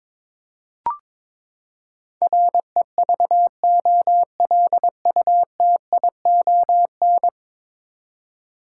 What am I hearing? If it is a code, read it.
Morse code, heard as a keyed tone, about 22 wpm, 700 Hz. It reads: REVOLUTION
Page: morse-code-ninja-practice-22wpm